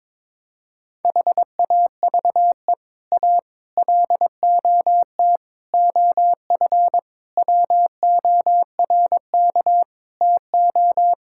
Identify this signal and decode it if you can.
Morse code, heard as a keyed tone, about 22 wpm, 700 Hz. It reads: HAVE A LOT OF WORK TO
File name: practice / 22wpm / Sets of 7 Words - 22wpm